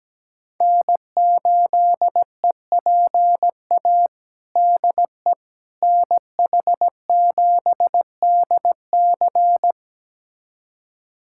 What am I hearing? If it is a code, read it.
Morse code, heard as a keyed tone, about 17 wpm, 700 Hz. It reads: N8EPA DE NH7DC
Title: USA Callsign DE Callsign - 5 Characters - 17wpm